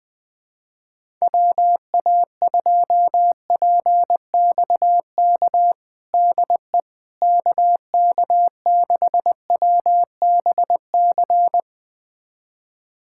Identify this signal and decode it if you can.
Morse code, heard as a keyed tone, about 20 wpm, 700 Hz. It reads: WA2PXK DE KK6WBC
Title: USA Callsign DE Callsign - 6 Characters - 20wpm